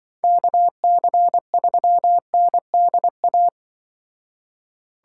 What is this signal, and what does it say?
Morse code, heard as a keyed tone, about 23 wpm, 700 Hz. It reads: KC3NDA
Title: Speed Racing - USA 2x3 Call Signs 20wpm